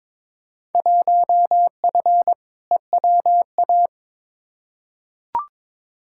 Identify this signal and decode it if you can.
Morse code, heard as a keyed tone, about 22 wpm, 700 Hz. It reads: 1F EWA
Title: Field Day Exchanges Encore - 22wpm